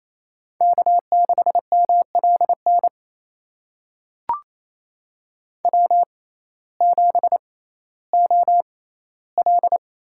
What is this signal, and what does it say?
Morse code, heard as a keyed tone, about 28 wpm, spaced as for 15 wpm, 700 Hz. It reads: K6MLN W7OL
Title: USA Call Signs - 5 Characters - 6x Character Spacing - 28wpm